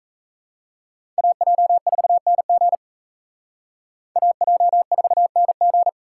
Morse code, heard as a keyed tone, about 35 wpm, 700 Hz. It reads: AJ4NG AJ4NG